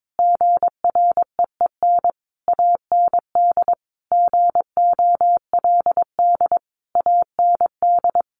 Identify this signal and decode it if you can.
Morse code, heard as a keyed tone, about 22 wpm, 700 Hz. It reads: GREEN AND GOLD AND
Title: A Sound Of Thunder 22wpm